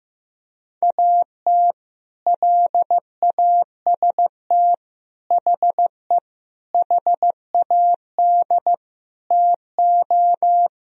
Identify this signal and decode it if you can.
Morse code, heard as a keyed tone, about 16 wpm, 700 Hz. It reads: AT LAST HE HAD TO
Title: Sentences from Top 1000 Words - Review Part 4 - 15wpm